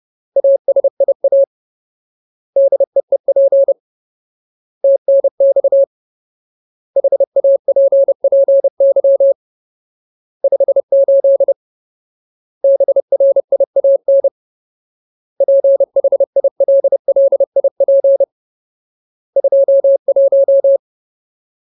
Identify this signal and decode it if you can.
Morse code, heard as a keyed tone, about 30 wpm, 550 Hz. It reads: ASIA DEEP TNX HAPPY 58 BRIAN PHILLIP 21
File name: Ultimate Word List / 30WPM Ultimate